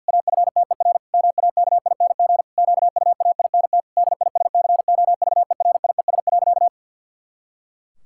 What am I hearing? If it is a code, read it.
Morse code, heard as a keyed tone, about 50 wpm, 700 Hz. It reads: AFTER MAKING QUAINT DISCOVERIES.
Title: Pangram Challenge v2 - 50wpm